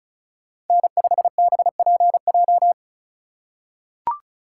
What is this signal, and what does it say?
Morse code, heard as a keyed tone, about 36 wpm, 700 Hz. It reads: N5BPJ
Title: USA Call Signs - 4x Character Spacing - 35wpm